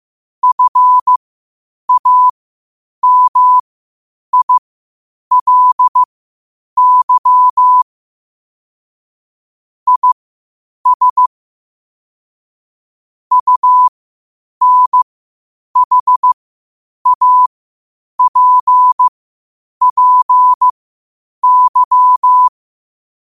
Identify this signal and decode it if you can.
Morse code, heard as a keyed tone, about 15 wpm, spaced as for 8 wpm, 1000 Hz. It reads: FAMILY IS UNHAPPY